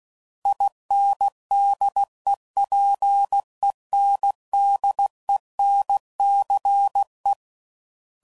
Morse code, heard as a keyed tone, about 16 wpm, 800 Hz. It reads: INDEPENDENCE
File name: CW Code Groups @15WPM / fromSM's / Cities